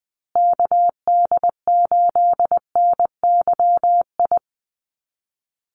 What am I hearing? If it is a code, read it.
Morse code, heard as a keyed tone, about 20 wpm, 700 Hz. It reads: KD8NYI
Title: USA Call Signs Repeated 3x 20wpm